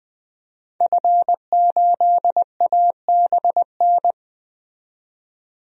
Morse code, heard as a keyed tone, about 21 wpm, 700 Hz. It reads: F8ABN
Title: DXCC-100 Call Signs - 5 Characters - 20wpm